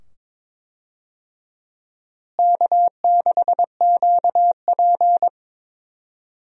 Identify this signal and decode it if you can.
Morse code, heard as a keyed tone, about 21 wpm, 700 Hz. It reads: K6QP
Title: USA Call Signs - 4 Characters - Repeated 2x - 22wpm